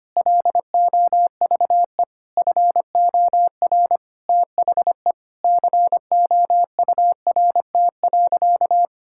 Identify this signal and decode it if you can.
Morse code, heard as a keyed tone, about 25 wpm, 700 Hz. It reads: LOVE FOR THE COURT.
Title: Sentences from Top 1600 Words 25wpm